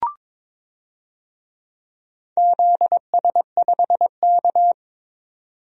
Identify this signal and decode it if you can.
Morse code, heard as a keyed tone, about 22 wpm, 700 Hz. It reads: ZS5K
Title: DXCC-100 Call Signs - 4 Characters - 22wpm